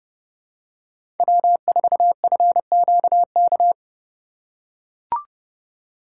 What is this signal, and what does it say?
Morse code, heard as a keyed tone, about 30 wpm, 700 Hz. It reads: W4FQK
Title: USA Call Signs - 5 Characters - 2x Character Spacing - 30wpm